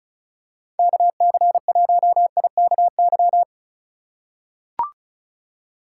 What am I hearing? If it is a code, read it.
Morse code, heard as a keyed tone, about 34 wpm, 700 Hz. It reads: KC1IKY
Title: USA Call Signs - 6 Characters - 4x Character Spacing - 35wpm